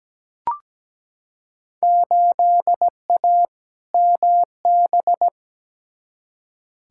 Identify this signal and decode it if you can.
Morse code, heard as a keyed tone, about 17 wpm, 700 Hz. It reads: 8A MB
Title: Field Day Exchanges Encore - 17wpm